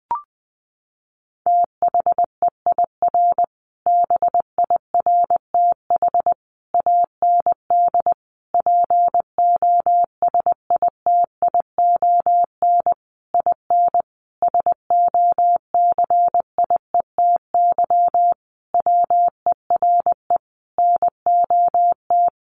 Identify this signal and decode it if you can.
Morse code, heard as a keyed tone, about 19 wpm, 700 Hz. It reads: THEIR BIRTH AND POSITION IN SOCIETY WERE NOT
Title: Sentences from Top 900 Words 20wpm